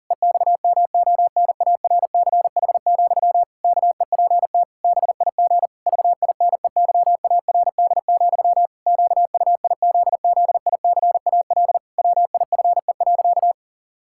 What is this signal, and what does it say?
Morse code, heard as a keyed tone, about 37 wpm, 700 Hz. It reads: EXMONARCH, KEPT BIG VINEYARD, QUIZZICAL WIFE.
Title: Pangram Challenge v2 - 40wpm